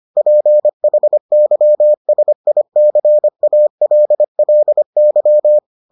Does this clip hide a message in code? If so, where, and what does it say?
Yes: Morse code from 0.2 to 5.6 s, heard as a keyed tone, about 25 wpm, 600 Hz. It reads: PHYSICALLY